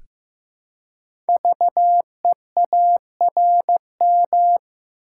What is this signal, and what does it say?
Morse code, heard as a keyed tone, about 15 wpm, 700 Hz. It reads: VEARM
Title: Code Groups - 5 Letters - 15wpm